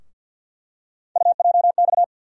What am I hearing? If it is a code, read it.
Morse code, heard as a keyed tone, about 50 wpm, 700 Hz. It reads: UJX
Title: USA Call Sign Postfixes - 3 Letters 50wpm